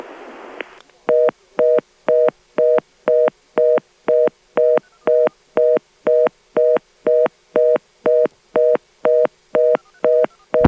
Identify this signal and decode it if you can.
Reorder tone (North America), 480 + 620 Hz.